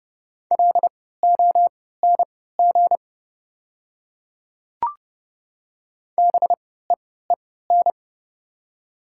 Morse code, heard as a keyed tone, about 30 wpm, spaced as for 17 wpm, 700 Hz. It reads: LONG BEEN
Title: 3x Character Spacing - Top 200 Words - 30wpm